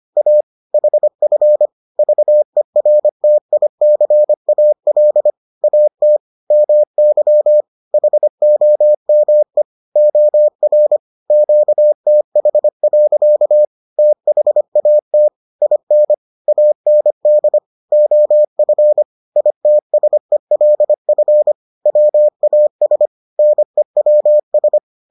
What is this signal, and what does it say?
Morse code, heard as a keyed tone, about 25 wpm, 600 Hz. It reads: A HF VERTICAL AT MY HOME OR QTH. THAT IN AND OF ITSELF WAS NEWS